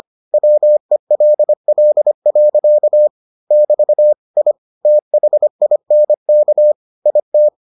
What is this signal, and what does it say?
Morse code, heard as a keyed tone, about 25 wpm, 600 Hz. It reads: WELL. <BT> I THINK IT